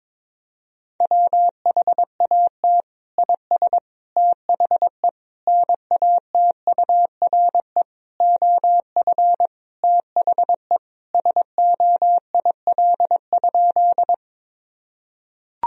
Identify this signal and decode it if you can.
Morse code, heard as a keyed tone, about 22 wpm, 700 Hz. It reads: WHAT IS THE NATURE OF THE SOIL?
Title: Sentences from Top 1000 Words - Review Part 4 - 22wpm